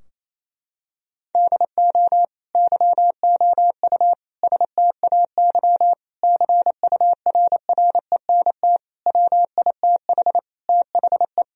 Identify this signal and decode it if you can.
Morse code, heard as a keyed tone, about 28 wpm, 700 Hz. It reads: DO YOU STAY CURRENT WITH THE
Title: Sentences from Top 1000 Words - Review Part 4 - 28wpm